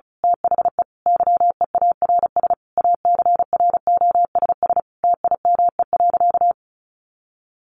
Morse code, heard as a keyed tone, about 34 wpm, 700 Hz. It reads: THE YEARS ACROSS TIME.
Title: A Sound Of Thunder 35wpm